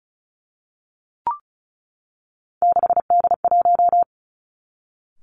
Morse code, heard as a keyed tone, about 36 wpm, 700 Hz. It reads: 6D1
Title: Three Challenging Letter-Number Combinations 35wpm